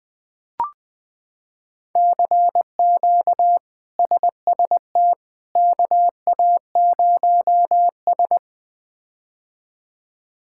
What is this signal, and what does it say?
Morse code, heard as a keyed tone, about 21 wpm, 700 Hz. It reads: CQ SST KA0S
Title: K1USN SST QSO Practice - 20wpm